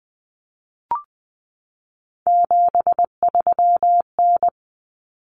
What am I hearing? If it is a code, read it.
Morse code, heard as a keyed tone, about 20 wpm, 700 Hz. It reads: 73N
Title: Common Cut Numbers - 3-Digit 20wpm